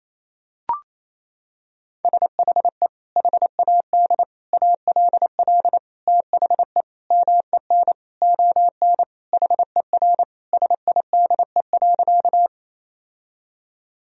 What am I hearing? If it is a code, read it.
Morse code, heard as a keyed tone, about 28 wpm, 700 Hz. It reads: SHE HAD ALL THE MEN ON HER SIDE.
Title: Sentences from Top 1000 Words - Review Part 1 - 28wpm